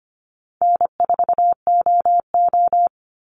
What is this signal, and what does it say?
Morse code, heard as a keyed tone, about 25 wpm, 700 Hz. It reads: N4OO
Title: Call Signs - 25wpm